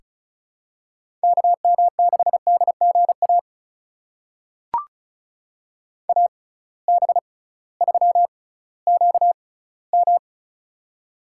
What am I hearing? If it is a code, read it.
Morse code, heard as a keyed tone, about 36 wpm, spaced as for 19 wpm, 700 Hz. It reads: KM6DGA AB3QM
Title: USA Call Signs - 6x Character Spacing - 35wpm